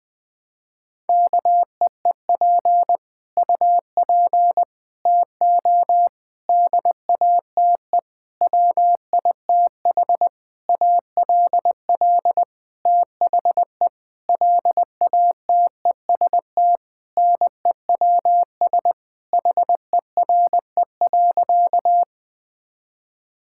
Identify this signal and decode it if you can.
Morse code, heard as a keyed tone, about 20 wpm, 700 Hz. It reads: KEEP UP TO DATE WITH ALL THE LATEST NEWS HERE.
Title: Sentences with 9-Word Phrases - 20wpm